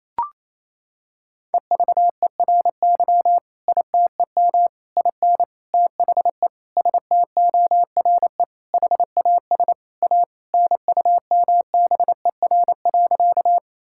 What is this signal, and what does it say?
Morse code, heard as a keyed tone, about 27 wpm, 700 Hz. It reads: EVERY ITEM IN THE STORE HAS A NUMBER.
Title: Sentences from Top 1000 Words - Review Part 6 - 28wpm